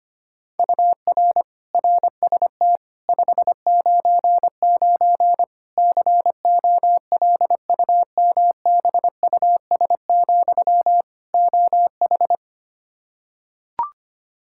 Morse code, heard as a keyed tone, about 25 wpm, 700 Hz. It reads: UR RST 599 COLUMBUS, OH
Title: K1USN SST QSO Practice 25wpm